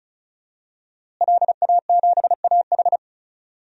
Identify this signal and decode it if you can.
Morse code, heard as a keyed tone, about 36 wpm, 700 Hz. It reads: LA7AH
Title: DXCC-100 Call Signs - 5 Characters - Repeated 2x - 35wpm